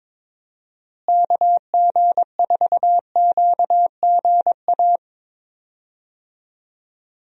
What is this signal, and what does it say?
Morse code, heard as a keyed tone, about 22 wpm, 700 Hz. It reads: KG4QGA
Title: USA Call Signs - 6 Characters - Repeated 3x - 22wpm